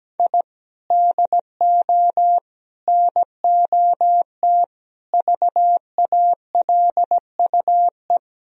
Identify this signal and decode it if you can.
Morse code, heard as a keyed tone, about 17 wpm, 700 Hz. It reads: I DO NOT VALUE
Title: Sentences from Top 1000 Words - Review Part 4 - 17wpm